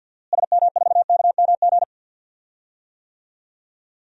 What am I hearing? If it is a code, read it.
Morse code, heard as a keyed tone, about 50 wpm, 700 Hz. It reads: SM4KMG
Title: DXCC-100 Call Signs - 6 Characters - Repeated 3x - 50wpm